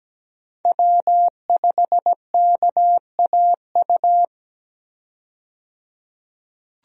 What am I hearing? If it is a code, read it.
Morse code, heard as a keyed tone, about 17 wpm, 700 Hz. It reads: W5KAU
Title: USA Call Signs - 5 Characters - 17wpm